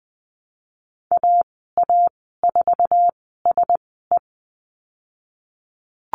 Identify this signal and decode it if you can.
Morse code, heard as a keyed tone, about 20 wpm, spaced as for 15 wpm, 700 Hz. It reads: AA4SE
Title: 055 Lesson 10g - Letter r - Call Signs 20wpm